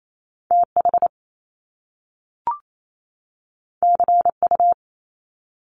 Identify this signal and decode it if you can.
Morse code, heard as a keyed tone, about 28 wpm, 700 Hz. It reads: TH CU
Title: Top Two Letter Combinations in Words 28wpm